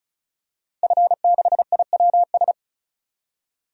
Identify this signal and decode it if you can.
Morse code, heard as a keyed tone, about 34 wpm, 700 Hz. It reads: F6IWS
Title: DXCC-100 Call Signs - 5 Characters - 35wpm